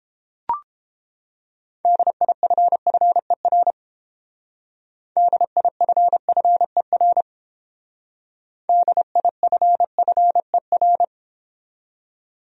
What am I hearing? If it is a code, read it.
Morse code, heard as a keyed tone, about 29 wpm, 700 Hz. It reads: DIFFER DIFFER DIFFER